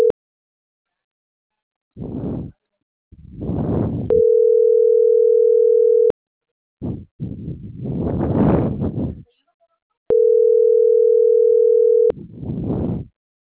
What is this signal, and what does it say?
Ringback tone (North America), 440 + 480 Hz.